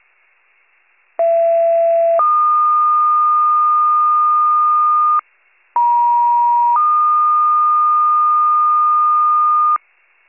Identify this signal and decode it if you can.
Two-tone page, heard as a one-second tone, then a three-second tone, about 670 then 1220 Hz, about 950 then 1220 Hz. It is heard from 1.2 to 9.8 s.
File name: pager / 83 RESCUE / wcalert